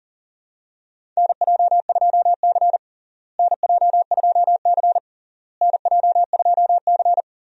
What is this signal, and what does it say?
Morse code, heard as a keyed tone, about 40 wpm, 700 Hz. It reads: NJ2C NJ2C NJ2C
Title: USA Call Signs - 4 Characters - Repeated 3x - 40wpm